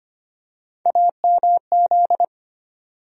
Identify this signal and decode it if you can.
Morse code, heard as a keyed tone, about 25 wpm, 700 Hz. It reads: AMZ
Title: USA Call Sign Postfixes - 3 Letters 25wpm